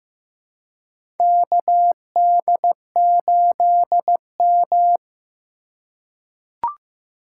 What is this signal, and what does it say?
Morse code, heard as a keyed tone, about 15 wpm, 700 Hz. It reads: KD8M
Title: USA Call Signs - 4 Characters - 2x Character Spacing - 15wpm